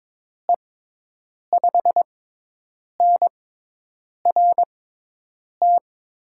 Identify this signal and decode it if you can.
Morse code, heard as a keyed tone, about 22 wpm, spaced as for 8 wpm, 700 Hz. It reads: E5NRT
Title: USA Call Signs - 6 Characters - 6x Character Spacing - 22wpm